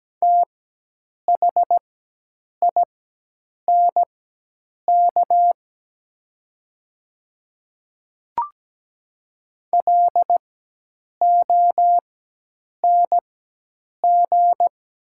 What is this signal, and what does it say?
Morse code, heard as a keyed tone, about 17 wpm, spaced as for 8 wpm, 700 Hz. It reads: THINK LONG
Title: 4x Character Spacing - Top 500 Words - 17wpm